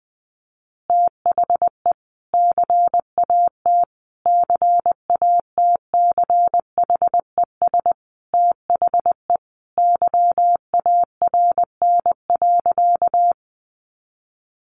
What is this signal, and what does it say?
Morse code, heard as a keyed tone, about 19 wpm, 700 Hz. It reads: THE CAT CATCHES THE YARN.
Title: Additional Easy Phrases 20wpm